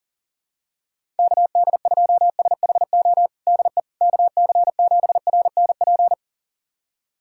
Morse code, heard as a keyed tone, about 40 wpm, 700 Hz. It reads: KD2SHO DE KC7RNP
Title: USA Callsign DE Callsign - 6 Characters - 40wpm